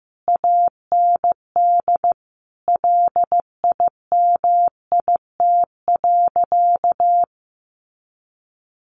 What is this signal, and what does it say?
Morse code, heard as a keyed tone, about 14 wpm, 700 Hz. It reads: AND LIMIT.
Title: Sentences from Top 1500 Words 15wpm